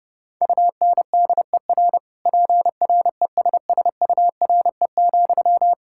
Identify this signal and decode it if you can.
Morse code, heard as a keyed tone, about 30 wpm, 700 Hz. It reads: UNDER PRESSURE,
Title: Sentences from Top 1000 Words - Review Part 6 - 30wpm